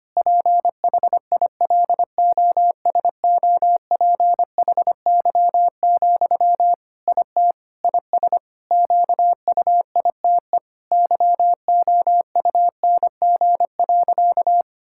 Morse code, heard as a keyed tone, about 25 wpm, 700 Hz. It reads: PHILOSOPHY, IT IS QUITE YOUNG.
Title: Sentences from Top 1000 Words - Review Part 6 - 25wpm